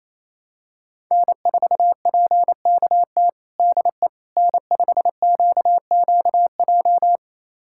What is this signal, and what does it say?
Morse code, heard as a keyed tone, about 27 wpm, 700 Hz. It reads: N4PKT DE N5QQJ
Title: USA Callsign DE Callsign - 5 Characters - 28wpm